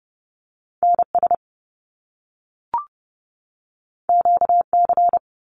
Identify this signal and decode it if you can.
Morse code, heard as a keyed tone, about 30 wpm, 700 Hz. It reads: NS QC
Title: Canadian Provinces and Territories 30wpm